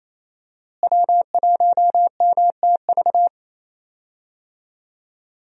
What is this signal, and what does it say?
Morse code, heard as a keyed tone, about 28 wpm, 700 Hz. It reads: W1MTV
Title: USA Call Signs - 5 Characters - Repeated 3x - 28wpm